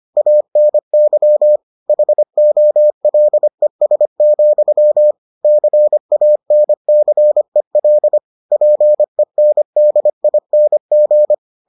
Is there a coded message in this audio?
Yes: Morse code from 0.2 to 11.3 s, heard as a keyed tone, about 25 wpm, 600 Hz. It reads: ANY HOLES, CANCEL PENDING